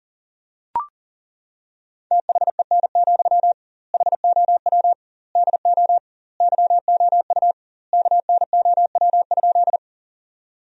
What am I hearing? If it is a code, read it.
Morse code, heard as a keyed tone, about 40 wpm, 700 Hz. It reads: THEN, HOW DO YOU KNOW?
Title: 2x Word-Spacing - Sentences from Top 100 Words - 40wpm